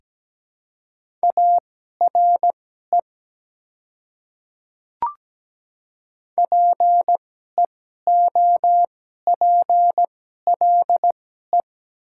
Morse code, heard as a keyed tone, about 17 wpm, spaced as for 12 wpm, 700 Hz. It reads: ARE PEOPLE
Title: 2x Character Spacing - Top 100 Words - 17wpm